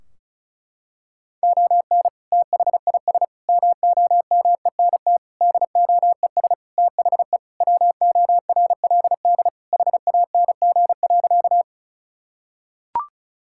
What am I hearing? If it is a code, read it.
Morse code, heard as a keyed tone, about 34 wpm, 700 Hz. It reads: ON THIS MOMENT DOES THE WORLD HANG.
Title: Sentences from Top 1600 Words 35wpm